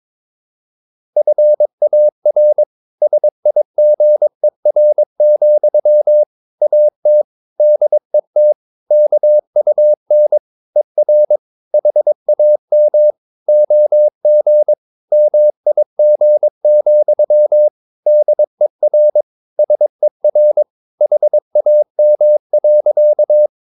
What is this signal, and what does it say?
Morse code, heard as a keyed tone, about 22 wpm, 600 Hz. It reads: FAR SIGER, AT DET KUN ER HAM OG MIG, DER SER HAM.
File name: Ferie QRS 22wpm